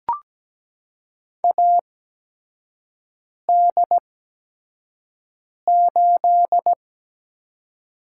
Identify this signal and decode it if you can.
Morse code, heard as a keyed tone, about 17 wpm, spaced as for 5 wpm, 700 Hz. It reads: AD8
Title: USA Call Signs - 4 Characters - 8x Character Spacing - 17wpm